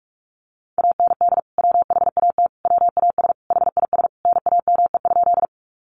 Morse code, heard as a keyed tone, about 44 wpm, 700 Hz. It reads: AND WHAT WAS HIS NAME?
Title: Sentences from Top 200 Words 45wpm